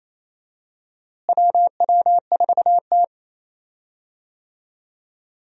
Morse code, heard as a keyed tone, about 28 wpm, 700 Hz. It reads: WW4T
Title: USA Call Signs - 4 Characters - Repeated 3x - 28wpm